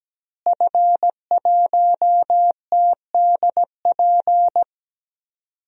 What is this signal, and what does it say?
Morse code, heard as a keyed tone, about 17 wpm, 700 Hz. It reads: F1TDP
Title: DXCC-100 Call Signs - 5 Characters - Repeated 2x - 17wpm